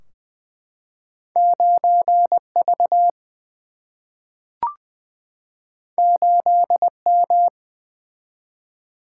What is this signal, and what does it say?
Morse code, heard as a keyed tone, about 20 wpm, 700 Hz. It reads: 9 V 8 M
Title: Single-Letter Single-Number 20wpm